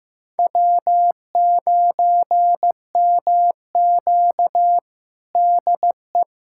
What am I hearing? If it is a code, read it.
Morse code, heard as a keyed tone, about 15 wpm, 700 Hz. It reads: W9MQ DE
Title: USA Callsign DE Callsign - 4 Characters - 15wpm